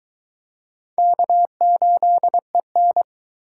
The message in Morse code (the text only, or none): K8EN